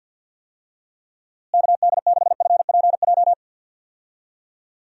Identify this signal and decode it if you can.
Morse code, heard as a keyed tone, about 51 wpm, 700 Hz. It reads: KD6FPJ
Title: USA Call Signs - 6 Characters - Repeated 3x - 50wpm